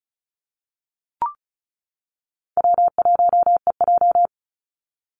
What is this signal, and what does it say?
Morse code, heard as a keyed tone, about 34 wpm, 700 Hz. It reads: W1EJ
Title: NA SOTA Activator Callsigns 35wpm